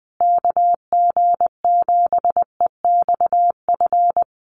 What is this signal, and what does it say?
Morse code, heard as a keyed tone, about 20 wpm, 700 Hz. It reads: KG7EXF
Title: USA Call Signs Repeated 3x 20wpm